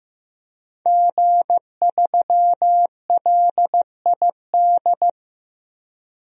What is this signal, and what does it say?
Morse code, heard as a keyed tone, about 15 wpm, 700 Hz. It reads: G3LID
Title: DXCC-100 Call Signs - 5 Characters - 15wpm